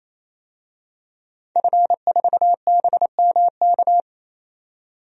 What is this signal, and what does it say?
Morse code, heard as a keyed tone, about 27 wpm, 700 Hz. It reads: F4BMK
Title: DXCC-100 Call Signs - 5 Characters - 28wpm